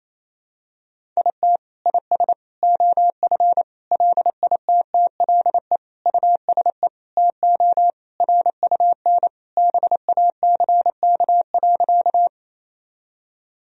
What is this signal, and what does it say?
Morse code, heard as a keyed tone, about 28 wpm, 700 Hz. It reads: IT IS OF LITTLE USE TO RUN BACK.
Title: Sentences from Top 1000 Words - Review Part 1 - 28wpm